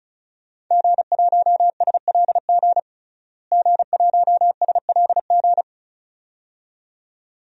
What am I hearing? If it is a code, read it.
Morse code, heard as a keyed tone, about 36 wpm, 700 Hz. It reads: G1SLG G1SLG
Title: DXCC-100 Call Signs - 5 Characters - Repeated 2x - 35wpm